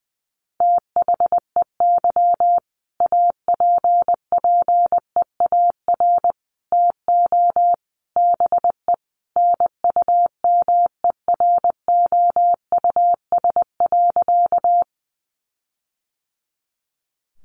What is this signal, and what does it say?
Morse code, heard as a keyed tone, about 20 wpm, 700 Hz. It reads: THEY APPEAR TO BE NUMEROUS.
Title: Sentences from Top 1400 Words 20wpm